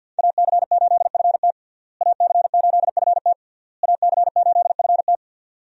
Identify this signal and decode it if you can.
Morse code, heard as a keyed tone, about 52 wpm, 700 Hz. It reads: AC8FT AC8FT AC8FT
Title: USA Call Signs - 5 Characters - Repeated 3x - 50wpm